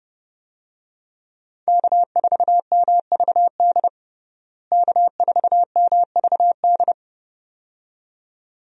Morse code, heard as a keyed tone, about 32 wpm, 700 Hz. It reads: K4MVD K4MVD
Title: USA Call Signs - 5 Characters - Repeated 2x - 30wpm